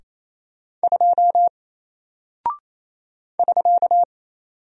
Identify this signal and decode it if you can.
Morse code, heard as a keyed tone, about 28 wpm, 700 Hz. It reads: 2 <SK>
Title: Single Character (Letter:Number:Punctuation:Prosign) Alt 28wpm